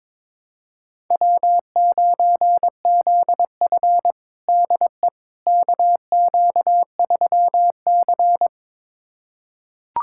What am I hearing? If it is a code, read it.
Morse code, heard as a keyed tone, about 21 wpm, 700 Hz. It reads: W9ZF DE KQ3C
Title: USA Callsign DE Callsign - 4 Characters - 22wpm